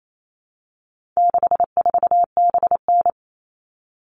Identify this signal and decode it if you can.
Morse code, heard as a keyed tone, about 28 wpm, 700 Hz. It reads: 64BN
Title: All Cut Numbers - 4-Digit 28wpm